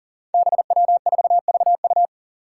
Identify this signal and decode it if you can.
Morse code, heard as a keyed tone, about 43 wpm, 700 Hz. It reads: BW4VU
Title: Five Challenging Letter-Number Combinations 40wpm